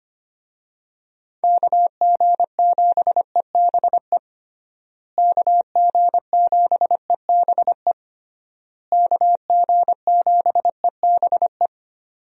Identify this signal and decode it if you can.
Morse code, heard as a keyed tone, about 25 wpm, 700 Hz. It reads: KG7EBE KG7EBE KG7EBE
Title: USA Call Signs - 6 Characters - Repeated 3x - 25wpm